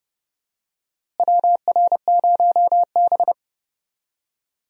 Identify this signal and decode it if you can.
Morse code, heard as a keyed tone, about 30 wpm, 700 Hz. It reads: WR0B
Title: USA Call Signs - 4 Characters - Repeated 2x - 30wpm